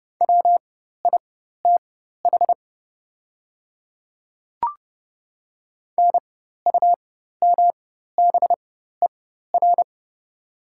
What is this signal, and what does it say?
Morse code, heard as a keyed tone, about 30 wpm, spaced as for 14 wpm, 690 Hz. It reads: WITH NUMBER
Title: 4x Character Spacing - Top 100 Words - 30wpm